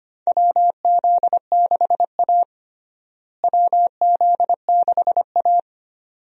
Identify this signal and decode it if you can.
Morse code, heard as a keyed tone, about 25 wpm, 700 Hz. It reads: WZ6A WZ6A
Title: USA Call Signs - 4 Characters - Repeated 2x - 25wpm